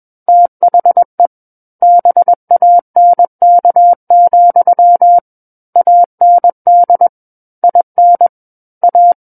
Morse code, heard as a keyed tone, about 21 wpm, 700 Hz. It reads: THE BANK, AND IN A
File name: Book One / 12-Chapter 12 - 21 WPM